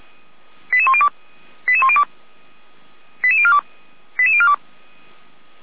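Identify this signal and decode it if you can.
Five-tone selective call (ZVEI), heard as quick runs of five tones, tones of about 70 ms. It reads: 88102 88102 88042 88042
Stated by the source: ZVEI five-tone selective call received on an SDR.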